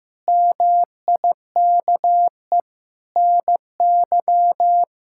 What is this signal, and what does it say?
Morse code, heard as a keyed tone, about 15 wpm, 700 Hz. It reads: MIKE NY
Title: K1USN SST QSO Practice - 15wpm